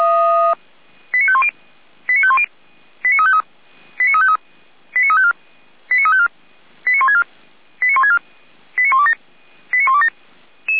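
Five-tone selective call (ZVEI), heard as quick runs of five tones, tones of about 70 ms. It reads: 89510 89510 89353 89353 89354 89354 89164 89164 89127 89127
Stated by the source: Five-tone ZVEI selective call received by SDR, fire brigade Feuerwehr Orlamünde.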